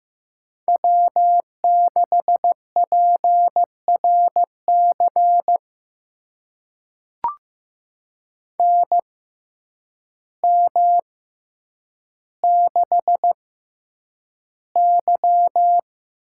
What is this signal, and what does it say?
Morse code, heard as a keyed tone, about 15 wpm, spaced as for 8 wpm, 700 Hz. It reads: W6PRC NM6Y
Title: USA Call Signs - 6x Character Spacing - 15wpm